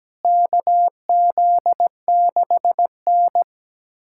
Morse code, heard as a keyed tone, about 17 wpm, 700 Hz. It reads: KZ6N
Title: USA Callsign DE Callsign - 4 Characters - 17wpm